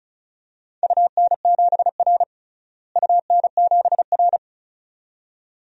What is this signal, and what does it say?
Morse code, heard as a keyed tone, about 34 wpm, 700 Hz. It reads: UN7R UN7R
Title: DXCC-100 Call Signs - 4 Characters - Repeated 2x - 35wpm